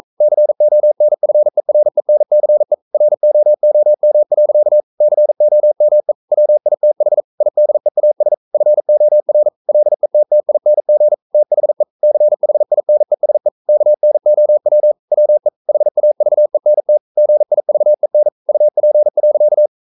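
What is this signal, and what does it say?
Morse code, heard as a keyed tone, about 42 wpm, 600 Hz. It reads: CONFERENCE ROOM. COME WITH IDEAS FOR LETTING THE CHINESE KNOW WE HAVENT GIVEN UP.
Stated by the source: Morse code practice at 41 wpm, from CW med Gnister.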